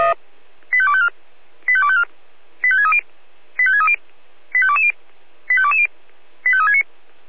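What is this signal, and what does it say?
Five-tone selective call (ZVEI), heard as quick runs of five tones, tones of about 70 ms. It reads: 85254 85254 85630 85630 85220 85220 85369
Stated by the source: ZVEI five-tone selective call received on SDR.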